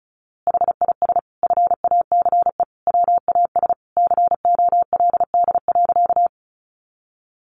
Morse code, heard as a keyed tone, about 34 wpm, 700 Hz. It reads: HIS FACE WAS COLD.
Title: A Sound Of Thunder 35wpm